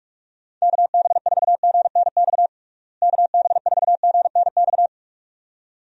Morse code, heard as a keyed tone, about 42 wpm, 700 Hz. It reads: KB4GNX KB4GNX
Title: USA Call Signs - 6 Characters - Repeated 2x - 45wpm